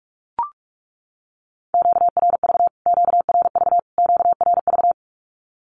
Morse code, heard as a keyed tone, about 44 wpm, 700 Hz. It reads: QRV QRV QRV
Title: Q-Codes Repeated 3x 45wpm